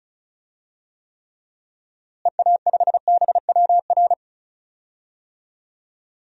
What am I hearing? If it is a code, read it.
Morse code, heard as a keyed tone, about 36 wpm, 700 Hz. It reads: EA5BWR
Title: DXCC-100 Call Signs - 6 Characters - Repeated 3x - 35wpm